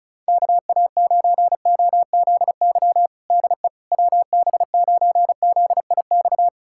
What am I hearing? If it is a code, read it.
Morse code, heard as a keyed tone, about 34 wpm, 700 Hz. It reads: KA9OZY DE WB9ZIX
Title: USA Callsign DE Callsign - 6 Characters - 35wpm